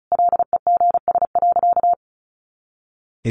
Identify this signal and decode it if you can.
Morse code, heard as a keyed tone, about 34 wpm, 700 Hz. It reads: LEGS.
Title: A Sound Of Thunder 35wpm